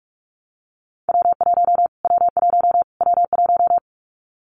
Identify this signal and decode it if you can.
Morse code, heard as a keyed tone, about 48 wpm, 700 Hz. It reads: W1 W1 W1
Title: Callsign Prefixes for Top 25 DXCC - Repeated 3x - 45wpm